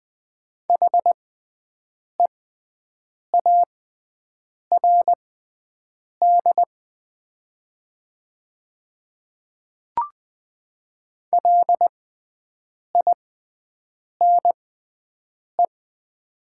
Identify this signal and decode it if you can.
Morse code, heard as a keyed tone, about 20 wpm, spaced as for 7 wpm, 690 Hz. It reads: HEARD LINE
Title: 6x Character Spacing - Top 500 Words - 20wpm